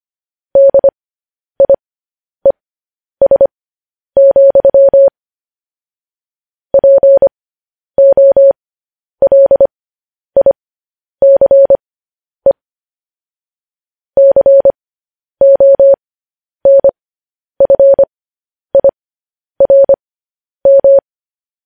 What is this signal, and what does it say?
Morse code, heard as a keyed tone, about 25 wpm, spaced as for 10 wpm, 550 Hz. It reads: DIES, POLICE CONFIRM
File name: podcast / News headlines-1x-25x10